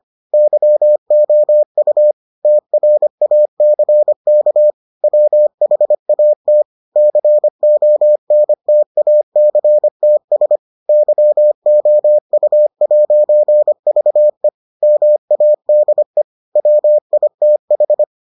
Morse code, heard as a keyed tone, about 25 wpm, 600 Hz. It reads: YOU TRACK WHAT CONTACTS YOU'VE MADE WITH